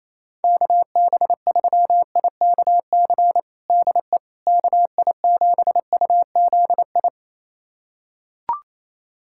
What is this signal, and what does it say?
Morse code, heard as a keyed tone, about 28 wpm, 700 Hz. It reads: KB3IKC DE KI7UZI
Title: USA Callsign DE Callsign - 6 Characters - 28wpm